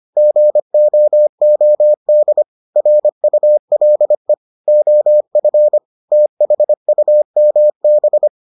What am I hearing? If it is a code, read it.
Morse code, heard as a keyed tone, about 25 wpm, 600 Hz. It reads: GOOD RULE OF THUMB